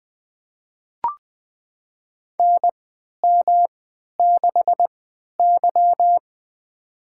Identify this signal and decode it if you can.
Morse code, heard as a keyed tone, about 20 wpm, spaced as for 11 wpm, 700 Hz. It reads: NM6Y
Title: USA Call Signs - 4 Characters - 3x Character Spacing - 20wpm